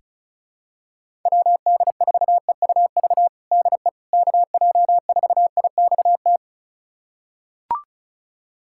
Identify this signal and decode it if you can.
Morse code, heard as a keyed tone, about 36 wpm, 700 Hz. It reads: WD4EUV DE KJ4IXT
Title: USA Callsign DE Callsign - 6 Characters - 35wpm